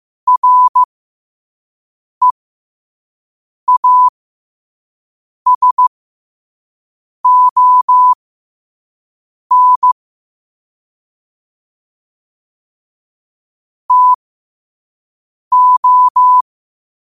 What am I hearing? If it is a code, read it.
Morse code, heard as a keyed tone, about 15 wpm, spaced as for 5 wpm, 1000 Hz. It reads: REASON TO